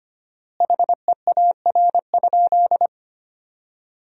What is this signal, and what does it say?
Morse code, heard as a keyed tone, about 25 wpm, 700 Hz. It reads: HEAR?
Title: 4x Word-Spacing - Sentences from Top 300 Words - 25wpm